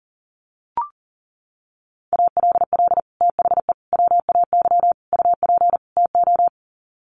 Morse code, heard as a keyed tone, about 40 wpm, 700 Hz. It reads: ALL THE WAY UP TO
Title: Sets of 5 Words Encore 40wpm